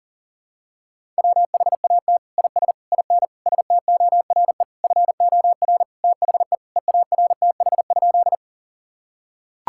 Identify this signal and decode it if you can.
Morse code, heard as a keyed tone, about 40 wpm, 700 Hz. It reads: WHAT IS IN STORE FOR THE EARTH?
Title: Sentences from Top 1000 Words - Review Part 5 - 40wpm